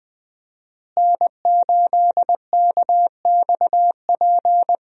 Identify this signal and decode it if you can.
Morse code, heard as a keyed tone, about 20 wpm, 700 Hz. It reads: N8KXP
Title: USA Call Signs - 5 Characters - Repeated 3x - 20wpm